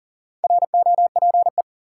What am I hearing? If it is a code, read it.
Morse code, heard as a keyed tone, about 40 wpm, 700 Hz. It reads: ROPE
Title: Four Letter Words - 40wpm